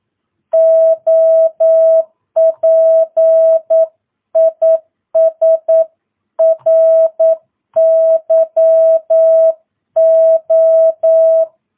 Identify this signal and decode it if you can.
Morse code, heard as a keyed tone, about 9 wpm, 650 Hz. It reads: OP IS RYO
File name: jh8cbh / 2022cw